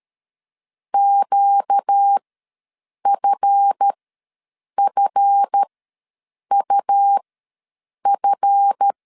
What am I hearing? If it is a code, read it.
Morse code, heard as a keyed tone, about 13 wpm, spaced as for 7 wpm, 800 Hz. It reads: QFFUF
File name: LEZIONI / 7 GF9Z / 7 GF ws 12wpm 3140